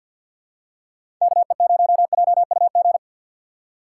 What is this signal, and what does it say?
Morse code, heard as a keyed tone, about 53 wpm, 700 Hz. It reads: KE0JUG
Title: USA Call Signs - 6 Characters - 3x Character Spacing - 50wpm